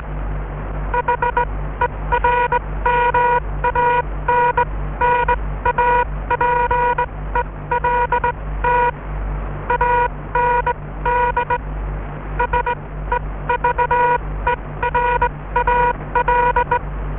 Morse code, heard as a keyed tone, about 20 wpm, spaced as for 16 wpm, 960 Hz. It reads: HERMANN APELT AND SEVERAL